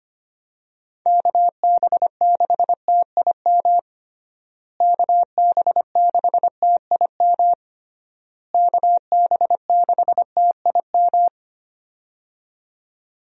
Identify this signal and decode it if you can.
Morse code, heard as a keyed tone, about 25 wpm, 700 Hz. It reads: KB6TIM KB6TIM KB6TIM
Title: USA Call Signs - 6 Characters - Repeated 3x - 25wpm